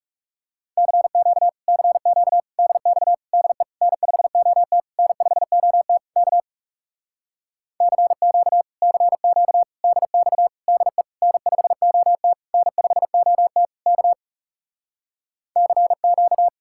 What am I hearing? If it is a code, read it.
Morse code, heard as a keyed tone, about 38 wpm, 700 Hz. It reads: CQ CQ DX DE N5OT N5OT K CQ CQ DX DE N5OT N5OT K CQ
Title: Speed-Racing Common QSO Phrases - 30wpm